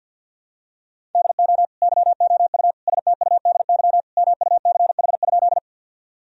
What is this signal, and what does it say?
Morse code, heard as a keyed tone, about 50 wpm, 700 Hz. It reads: DO YOU STUDY MUCH?
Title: Sentences from Top 1000 Words - Review Part 1 - 50wpm